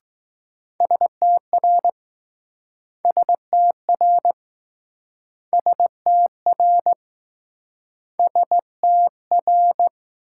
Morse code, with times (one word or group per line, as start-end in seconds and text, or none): 0.8-1.9 STR
3.0-4.3 STR
5.5-6.9 STR
8.2-9.9 STR